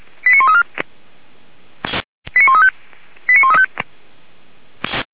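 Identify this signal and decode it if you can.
Five-tone selective call (ZVEI), heard as quick runs of five tones, tones of about 70 ms. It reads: 89134 89135 89135
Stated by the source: ZVEI five-tone call 89133 received on SDR by feuerwehr-orlamuende fire brigade.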